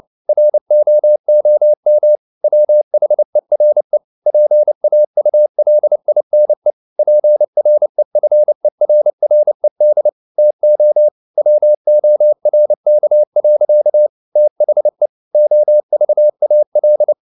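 Morse code, heard as a keyed tone, about 29 wpm, 600 Hz. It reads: ROOM WHERE PAULINE PREFERRED TO WORK. THE OVAL
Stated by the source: Morse code at 29 wpm, sending Ken Follett's novel Never in English, from CW med Gnister.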